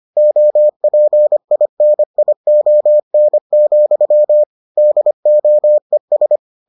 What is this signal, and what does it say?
Morse code, heard as a keyed tone, about 25 wpm, 600 Hz. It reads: OPINION, DOES